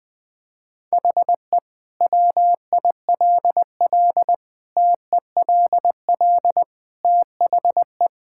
Morse code, heard as a keyed tone, about 21 wpm, 700 Hz. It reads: HE WILL TELL THE
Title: Sentences from Top 1000 Words - Review Part 5 - 20wpm